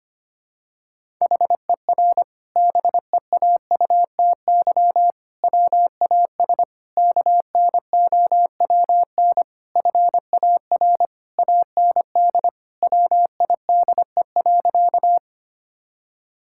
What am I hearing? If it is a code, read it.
Morse code, heard as a keyed tone, about 25 wpm, 700 Hz. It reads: HER BEAUTY WAS KNOWN FAR AND WIDE.
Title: Sentences from Top 1000 Words - Review Part 6 - 25wpm